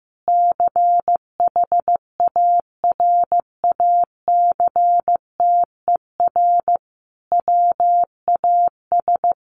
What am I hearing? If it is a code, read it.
Morse code, heard as a keyed tone, about 15 wpm, 700 Hz. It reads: CHARACTER WAS
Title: Sentences from Top 1200 Words 15wpm